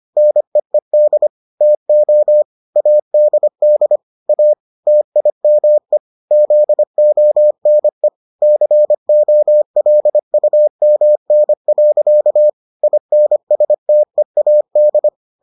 Morse code, heard as a keyed tone, about 25 wpm, 600 Hz. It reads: NEED TO ADD A TIME ZONE COLUMN. INSTEAD